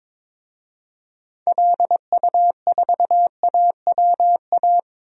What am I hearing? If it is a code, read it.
Morse code, heard as a keyed tone, about 22 wpm, 700 Hz. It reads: LU4AWA
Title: DXCC-100 Call Signs - 6 Characters - Repeated 3x - 22wpm